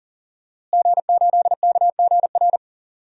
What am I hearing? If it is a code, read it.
Morse code, heard as a keyed tone, about 40 wpm, 700 Hz. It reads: G8KGR
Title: DXCC-100 Call Signs - 5 Characters - Repeated 2x - 40wpm